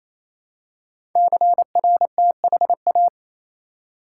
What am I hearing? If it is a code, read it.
Morse code, heard as a keyed tone, about 27 wpm, 700 Hz. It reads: CRTHA
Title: Code Groups - 5 Letters - 28wpm